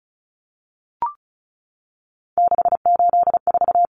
Morse code, heard as a keyed tone, about 34 wpm, 700 Hz. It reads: 684
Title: Numbers - 3-Digit 35wpm